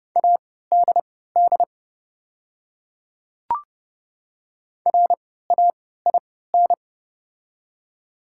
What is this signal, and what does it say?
Morse code, heard as a keyed tone, about 30 wpm, spaced as for 17 wpm, 690 Hz. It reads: ADD RAIN
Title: 3x Character Spacing - Top 500 Words - 30wpm